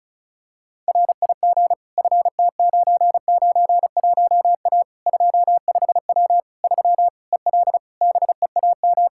Morse code, heard as a keyed tone, about 34 wpm, 700 Hz. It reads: RIG FT991A 25W 3 EL BEAM
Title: K1USN SST QSO Practice 35wpm